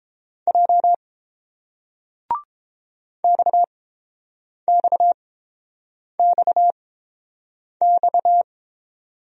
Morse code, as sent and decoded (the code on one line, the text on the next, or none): .--- / -..- -..- -..- -..-
J XXXX